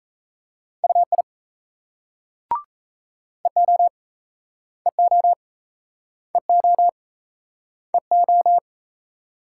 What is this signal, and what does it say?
Morse code, heard as a keyed tone, about 36 wpm, 700 Hz. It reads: UI EO EO EO EO